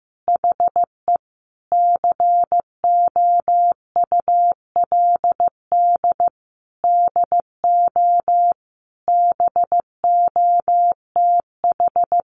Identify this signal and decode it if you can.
Morse code, heard as a keyed tone, about 15 wpm, 700 Hz. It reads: HE COULD DO BOTH
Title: Sentences from Top 1300 Words 15wpm